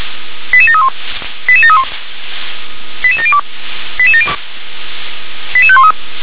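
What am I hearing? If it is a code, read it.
Five-tone selective call (ZVEI), heard as quick runs of five tones, tones of about 70 ms. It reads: 88731 88731 88702 88513
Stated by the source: Five-tone ZVEI selective call received on an SDR.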